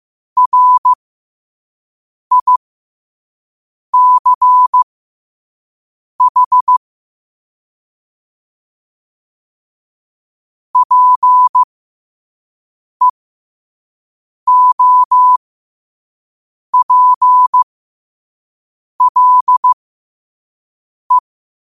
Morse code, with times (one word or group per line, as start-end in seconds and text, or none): 0.4-6.8 RICH
10.8-21.2 PEOPLE